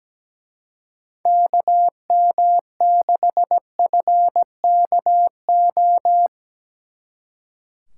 Morse code, heard as a keyed tone, about 19 wpm, 700 Hz. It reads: KM6FKO
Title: Speed Racing - USA Call Signs 17wpm